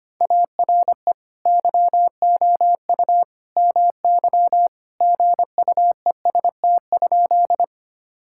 Morse code, heard as a keyed tone, about 25 wpm, 700 Hz. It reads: ARE YOU MY GUEST?
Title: Sentences from Top 1600 Words 25wpm